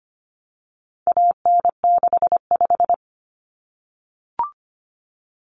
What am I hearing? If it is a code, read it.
Morse code, heard as a keyed tone, about 25 wpm, 700 Hz. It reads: AN65
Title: Common Cut Numbers - 4-Digit 25wpm